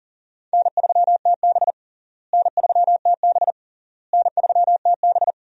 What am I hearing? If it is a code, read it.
Morse code, heard as a keyed tone, about 40 wpm, 700 Hz. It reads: N3TB N3TB N3TB
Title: USA Call Signs - 4 Characters - Repeated 3x - 40wpm